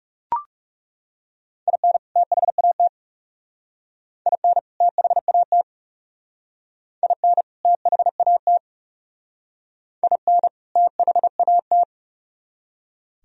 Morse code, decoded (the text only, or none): IN THAT IN THAT IN THAT IN THAT